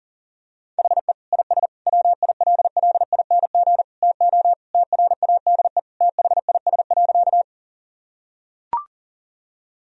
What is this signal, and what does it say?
Morse code, heard as a keyed tone, about 40 wpm, 700 Hz. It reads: HE IS WILLING TO TRADE THIS.
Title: Sentences from Top 1000 Words - Review Part 4 - 40wpm